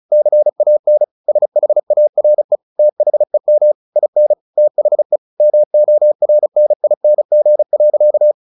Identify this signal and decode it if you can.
Morse code, heard as a keyed tone, about 34 wpm, 600 Hz. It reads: CAN SHARE THEM IN THE MORNING.
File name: the old man and the sea / the old man and the sea 1 35wpm